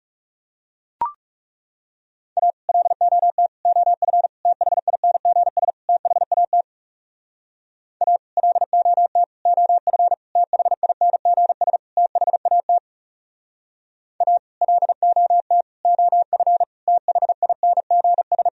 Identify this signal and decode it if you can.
Morse code, heard as a keyed tone, about 37 wpm, 700 Hz. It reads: A LOT OF THINGS THAT A LOT OF THINGS THAT A LOT OF THINGS
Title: Speed-Racing - Sets of 5 Words 30wpm